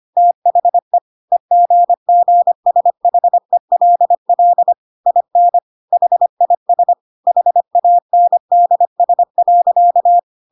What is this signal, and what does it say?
Morse code, heard as a keyed tone, about 25 wpm, 700 Hz. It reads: THE EGGSHELL IN HIS HANDS.
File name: Pinocchio / Pinocchio ch5 25 25 ik1hge